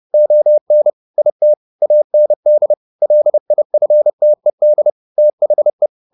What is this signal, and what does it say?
Morse code, heard as a keyed tone, about 30 wpm, 600 Hz. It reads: ON IT AND LIFTED THE